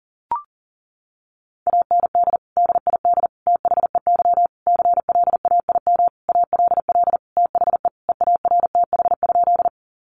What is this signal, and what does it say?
Morse code, heard as a keyed tone, about 40 wpm, 700 Hz. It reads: AND DID THEY CLAIM ALL THE EARTH?
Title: Sentences from Top 1500 Words 40wpm